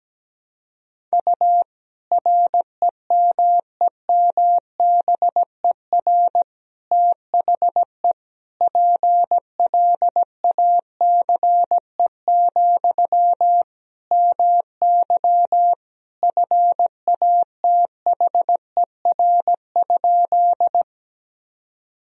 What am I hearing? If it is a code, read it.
Morse code, heard as a keyed tone, about 17 wpm, 700 Hz. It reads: U REMEMBER THE PLACE, MY FATHER?
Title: Sentences from Top 400 Words and Abbreviations 17wpm